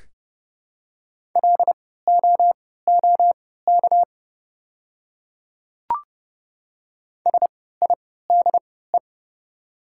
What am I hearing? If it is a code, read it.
Morse code, heard as a keyed tone, about 30 wpm, spaced as for 17 wpm, 700 Hz. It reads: LOOK SIDE
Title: 3x Character Spacing - Top 200 Words - 30wpm